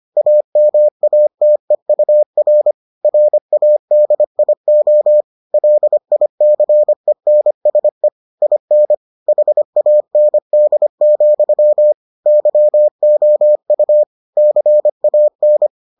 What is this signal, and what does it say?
Morse code, heard as a keyed tone, about 25 wpm, 600 Hz. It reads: AMATEUR RADIO LICENSE IN HAND, YOU CAN